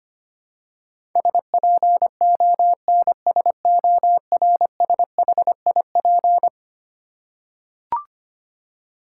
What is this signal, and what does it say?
Morse code, heard as a keyed tone, about 25 wpm, 700 Hz. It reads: SPONSORSHIP
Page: morse-code-ninja-practice-25wpm